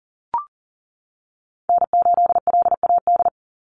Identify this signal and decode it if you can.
Morse code, heard as a keyed tone, about 40 wpm, 700 Hz. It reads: N8LAD
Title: USA Call Signs 40wpm